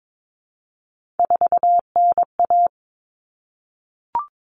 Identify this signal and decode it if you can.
Morse code, heard as a keyed tone, about 22 wpm, 700 Hz. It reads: 4NA
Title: Common Cut Numbers - 3-Digit 22wpm